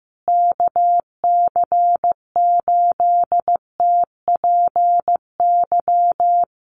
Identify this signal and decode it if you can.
Morse code, heard as a keyed tone, about 14 wpm, 700 Hz. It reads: KC8TPY
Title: USA Call Signs Repeated 3x 15wpm